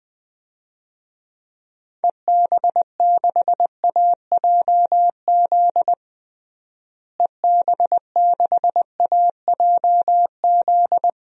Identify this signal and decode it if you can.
Morse code, heard as a keyed tone, about 21 wpm, 700 Hz. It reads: EB6AJZ EB6AJZ
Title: DXCC-100 Call Signs - 6 Characters - Repeated 3x - 20wpm